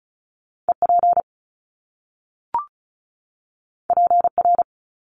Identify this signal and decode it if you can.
Morse code, heard as a keyed tone, about 36 wpm, 700 Hz. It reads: EP PR
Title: Top Two Letter Combinations in Words 35wpm